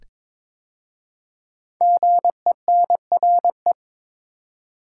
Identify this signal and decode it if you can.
Morse code, heard as a keyed tone, about 22 wpm, 700 Hz. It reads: GENRE
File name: 5 Letter Words - First 3 Letters Given - 22wpm